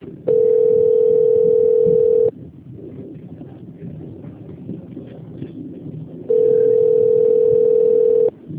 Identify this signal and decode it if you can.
Ringback tone (North America), 440 + 480 Hz.